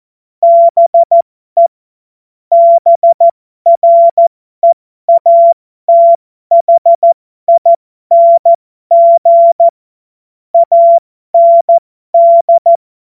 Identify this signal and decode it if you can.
Morse code, heard as a keyed tone, about 14 wpm, spaced as for 12 wpm, 680 Hz. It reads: BE BREATHING AND
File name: Morsekurs / Witze